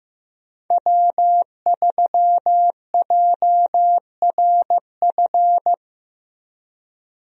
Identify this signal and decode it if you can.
Morse code, heard as a keyed tone, about 15 wpm, 700 Hz. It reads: W3JRF
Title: USA Call Signs - 5 Characters - Repeated 2x - 15wpm